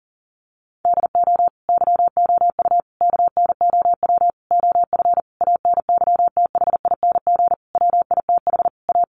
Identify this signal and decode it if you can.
Morse code, heard as a keyed tone, about 40 wpm, 700 Hz. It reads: DO YOU KNOW OF ANYTHING WITH A
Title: Sentences from Top 1200 Words 40wpm